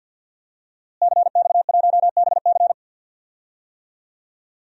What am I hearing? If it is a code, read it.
Morse code, heard as a keyed tone, about 51 wpm, 700 Hz. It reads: CX1BC
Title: DXCC-100 Call Signs - 5 Characters - Repeated 2x - 50wpm